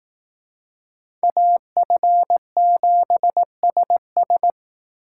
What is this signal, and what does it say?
Morse code, heard as a keyed tone, about 18 wpm, 700 Hz. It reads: AF7SS
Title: Speed Racing - USA 2x2 Call Signs 15wpm